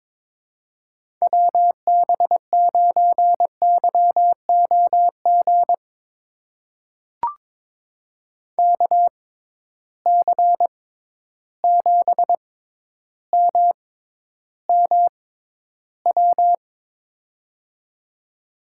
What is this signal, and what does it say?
Morse code, heard as a keyed tone, about 22 wpm, spaced as for 11 wpm, 700 Hz. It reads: WB9YOG KC7MMW
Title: USA Call Signs - 6 Characters - 6x Character Spacing - 22wpm